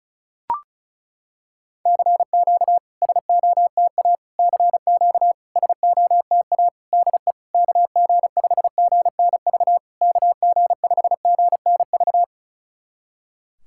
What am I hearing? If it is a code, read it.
Morse code, heard as a keyed tone, about 36 wpm, 690 Hz. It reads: CQ SOTA CQ SOTA DE KG5GNV KG5GNV
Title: SOTA QSO Practice 35wpm